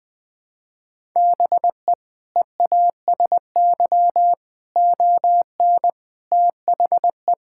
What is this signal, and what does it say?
Morse code, heard as a keyed tone, about 20 wpm, 700 Hz. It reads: BE EASY ON THE
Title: Sentences from Top 1600 Words 20wpm